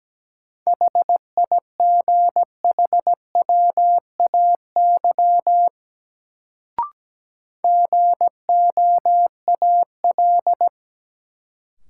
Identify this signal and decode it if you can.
Morse code, heard as a keyed tone, about 17 wpm, 700 Hz. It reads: HIGHWAY GOAL
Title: Sentences from Top 1400 Words - New Word Drill Alt - 17wpm